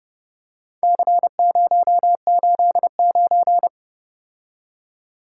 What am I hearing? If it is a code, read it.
Morse code, heard as a keyed tone, about 30 wpm, 700 Hz. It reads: C089